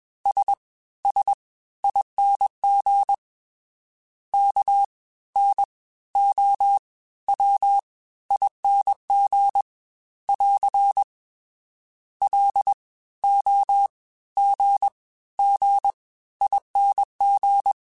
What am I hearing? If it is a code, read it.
Morse code, heard as a keyed tone, about 21 wpm, spaced as for 14 wpm, 800 Hz. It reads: SSING KNOWING <AR> LOGGING